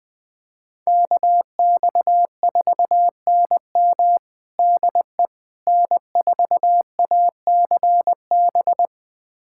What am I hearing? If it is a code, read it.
Morse code, heard as a keyed tone, about 20 wpm, 690 Hz. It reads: KX4NM DE N4ACB
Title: USA Callsign DE Callsign 20wpm